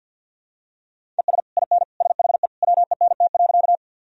Morse code, heard as a keyed tone, about 52 wpm, 700 Hz. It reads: ES IN SHE WENT.
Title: Sentences from Top 200 Words and Abbreviations 50wpm